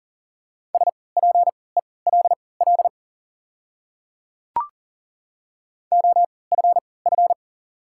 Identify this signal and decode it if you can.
Morse code, heard as a keyed tone, about 38 wpm, spaced as for 22 wpm, 700 Hz. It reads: SPELL OFF
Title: 3x Character Spacing - Top 500 Words - 40wpm